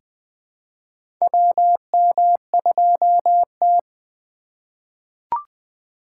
Morse code, heard as a keyed tone, about 20 wpm, 700 Hz. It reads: WM2T
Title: USA Call Signs - 4x Character Spacing - 20wpm